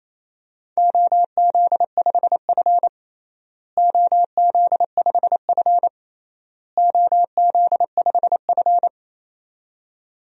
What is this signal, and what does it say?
Morse code, heard as a keyed tone, about 28 wpm, 700 Hz. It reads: OZ5F OZ5F OZ5F
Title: DXCC-100 Call Signs - 4 Characters - Repeated 3x - 28wpm